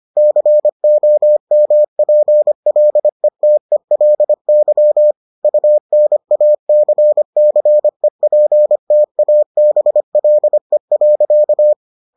Morse code, heard as a keyed tone, about 25 wpm, 600 Hz. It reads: COMPLETELY UNACCEPTABLE.